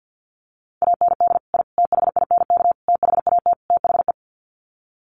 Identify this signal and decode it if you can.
Morse code, heard as a keyed tone, about 50 wpm, 700 Hz. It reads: AND I THINK THAT THE
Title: Sets of 5 Words 50wpm